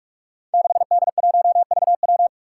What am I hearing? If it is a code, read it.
Morse code, heard as a keyed tone, about 41 wpm, 700 Hz. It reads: 6D1VW
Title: Five Challenging Letter-Number Combinations 45wpm